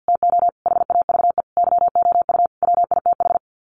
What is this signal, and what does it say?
Morse code, heard as a keyed tone, about 50 wpm, 700 Hz. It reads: TO HAVE YOU WITH
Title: Sets of 5 Words Encore2 50wpm